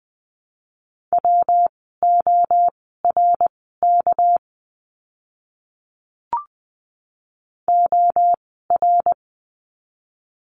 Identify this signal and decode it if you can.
Morse code, heard as a keyed tone, about 20 wpm, spaced as for 14 wpm, 700 Hz. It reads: WORK OR
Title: 199 Lesson 32f - Letter k - Words 20wpm